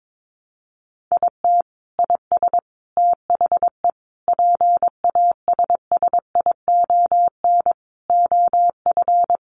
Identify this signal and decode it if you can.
Morse code, heard as a keyed tone, about 22 wpm, 700 Hz. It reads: IT IS THE PASSION OF
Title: Sentences from Top 1300 Words 22wpm